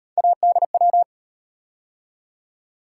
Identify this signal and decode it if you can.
Morse code, heard as a keyed tone, about 38 wpm, 700 Hz. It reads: ADW